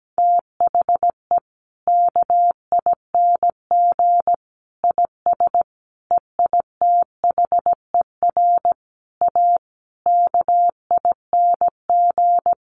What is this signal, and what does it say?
Morse code, heard as a keyed tone, about 17 wpm, 700 Hz. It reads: THE KING IS EITHER A KING
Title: Sentences from Top 600 Words 17wpm